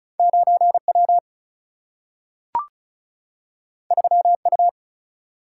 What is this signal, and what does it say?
Morse code, heard as a keyed tone, about 36 wpm, 700 Hz. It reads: 9 W 3 U
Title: Single-Letter Single-Number 35wpm